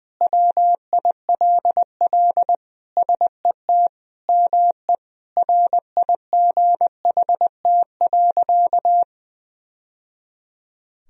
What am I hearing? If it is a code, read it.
Morse code, heard as a keyed tone, about 21 wpm, 700 Hz. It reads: WILL SET ME RIGHT.
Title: Sentences from Top 1000 Words - Review Part 1 - 20wpm